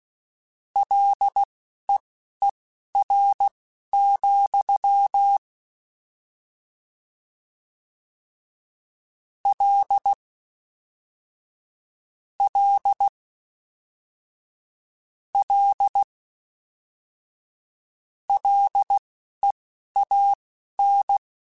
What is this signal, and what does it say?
Morse code, heard as a keyed tone, about 16 wpm, spaced as for 12 wpm, 800 Hz. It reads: LEER, L L L LEAN